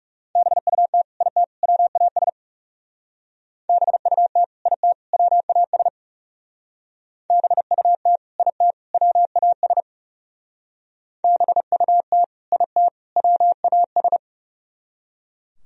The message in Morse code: BUT IT WAS BUT IT WAS BUT IT WAS BUT IT WAS

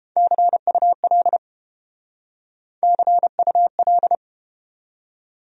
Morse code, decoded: CUL CUL

